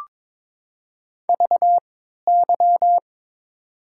Morse code, heard as a keyed tone, about 22 wpm, spaced as for 12 wpm, 700 Hz. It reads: VY
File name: Top 90 QSO Elements - 3x Character Spacing - 22wpm